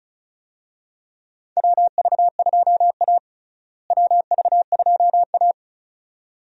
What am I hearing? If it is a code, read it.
Morse code, heard as a keyed tone, about 36 wpm, 700 Hz. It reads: WV2A WV2A
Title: USA Call Signs - 4 Characters - Repeated 2x - 35wpm